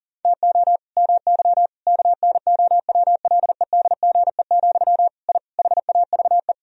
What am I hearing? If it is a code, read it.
Morse code, heard as a keyed tone, about 40 wpm, 700 Hz. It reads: TO MY KNOWLEDGE, I HAVE
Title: Sentences from Top 1000 Words - Review Part 3 - 40wpm